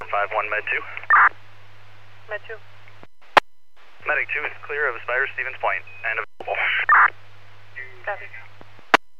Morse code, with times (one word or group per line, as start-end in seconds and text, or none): none